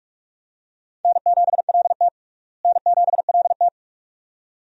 Morse code, heard as a keyed tone, about 46 wpm, 700 Hz. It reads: N7LT N7LT
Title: USA Call Signs - 4 Characters - Repeated 2x - 45wpm